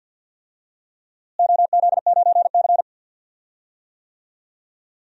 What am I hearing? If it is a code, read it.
Morse code, heard as a keyed tone, about 47 wpm, 700 Hz. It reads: OZ9C
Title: DXCC-100 Call Signs - 4 Characters - Repeated 2x - 50wpm